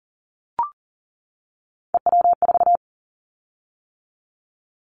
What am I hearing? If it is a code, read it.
Morse code, heard as a keyed tone, about 37 wpm, 700 Hz. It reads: EW4
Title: Callsign Prefixes for Top 100 DXCC - 40wpm